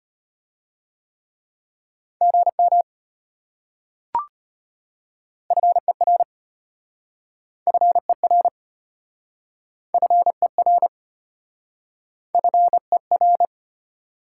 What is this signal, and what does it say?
Morse code, heard as a keyed tone, about 33 wpm, 700 Hz. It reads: GM FER FER FER FER